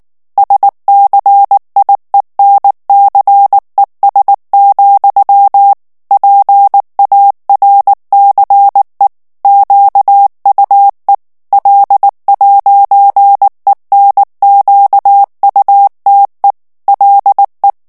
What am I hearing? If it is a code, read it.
Morse code, heard as a keyed tone, about 19 wpm, 800 Hz. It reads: SCIENCES, PARCE QUE L'ENQUTE LE
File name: morse mp3 / FRANCAIS TEXTE CLAIR 1 VITESSE 18 MOTS MINUTES